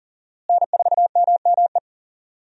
Morse code, heard as a keyed tone, about 38 wpm, 700 Hz. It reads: N4MME
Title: USA Call Signs - 5 Characters - Repeated 3x - 40wpm